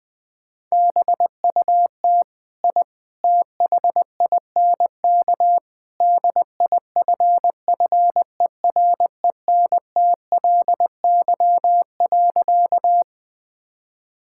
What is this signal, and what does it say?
Morse code, heard as a keyed tone, about 20 wpm, 700 Hz. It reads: BUT I THINK DIFFERENTLY.
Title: Sentences with 3-Word Phrases - 20wpm